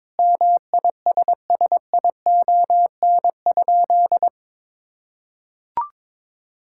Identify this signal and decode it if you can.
Morse code, heard as a keyed tone, about 22 wpm, 700 Hz. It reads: MISSION?
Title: Sentences from Top 1600 Words 22wpm